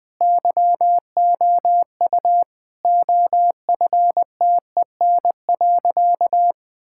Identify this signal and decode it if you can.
Morse code, heard as a keyed tone, about 20 wpm, 700 Hz. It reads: YOU OFTEN.
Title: Sentences from Top 1000 Words - Review Part 6 - 20wpm